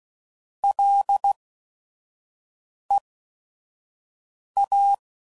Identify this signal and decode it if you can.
Morse code, heard as a keyed tone, about 16 wpm, spaced as for 5 wpm, 800 Hz. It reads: LEA